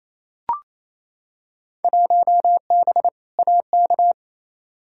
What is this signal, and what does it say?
Morse code, heard as a keyed tone, about 28 wpm, 700 Hz. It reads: 1B AK
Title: Field Day Exchanges Encore - 28wpm